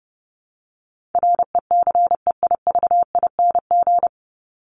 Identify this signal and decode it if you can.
Morse code, heard as a keyed tone, about 30 wpm, 700 Hz. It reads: RECEIVING
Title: ING Suffix - Top 100 Words with Base Words 30wpm